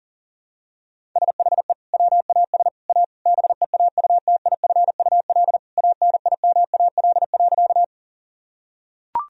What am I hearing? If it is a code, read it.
Morse code, heard as a keyed tone, about 40 wpm, 700 Hz. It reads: SHE WAS A BEAUTIFUL ANIMAL.
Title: Sentences from Top 1000 Words - Review Part 1 - 40wpm